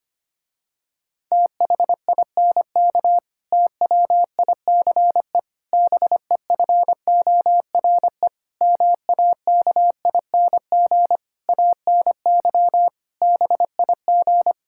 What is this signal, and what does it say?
Morse code, heard as a keyed tone, about 25 wpm, 700 Hz. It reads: THINK TWICE BEFORE MAKING ANY BIG
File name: practice / 25wpm / Difficult Sentences with Hints - 25wpm